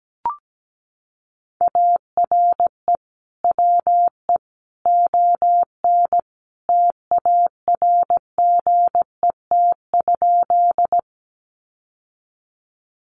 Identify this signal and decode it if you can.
Morse code, heard as a keyed tone, about 17 wpm, 700 Hz. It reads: ARE WE ON TARGET?
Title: Sentences from Top 1200 Words 17wpm